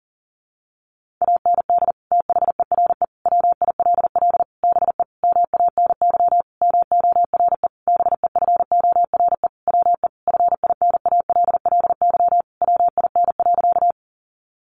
Morse code, heard as a keyed tone, about 40 wpm, 700 Hz. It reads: AND THERE WILL BE MANY MORE BEFORE WE FINALLY WIN.
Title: Sentences from Top 1500 Words 40wpm